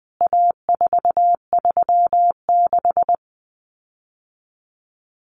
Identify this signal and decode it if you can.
Morse code, heard as a keyed tone, about 20 wpm, 700 Hz. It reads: A436
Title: CWT Exchange 20wpm